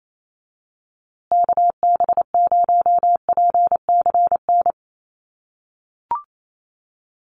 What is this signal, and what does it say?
Morse code, heard as a keyed tone, about 28 wpm, 700 Hz. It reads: KB0PCN
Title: USA Call Signs Repeated 2x 28wpm